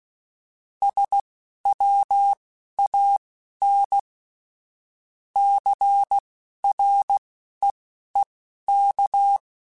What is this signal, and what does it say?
Morse code, heard as a keyed tone, about 16 wpm, spaced as for 11 wpm, 800 Hz. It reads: SWAN CREEK